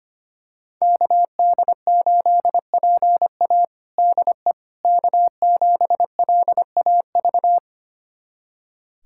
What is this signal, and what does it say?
Morse code, heard as a keyed tone, about 25 wpm, 700 Hz. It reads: KD8PA DE K7LAV
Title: USA Callsign DE Callsign - 5 Characters - 25wpm